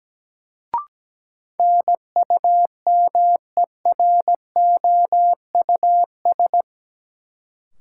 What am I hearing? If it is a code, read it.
Morse code, heard as a keyed tone, about 17 wpm, 700 Hz. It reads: NUMEROUS
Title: Sentences from Top 1400 Words - New Word Drill - 17wpm